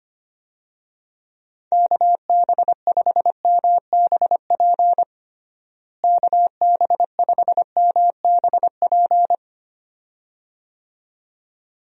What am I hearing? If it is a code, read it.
Morse code, heard as a keyed tone, about 25 wpm, 700 Hz. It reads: KB5MBP KB5MBP
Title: USA Call Signs - 6 Characters - Repeated 2x - 25wpm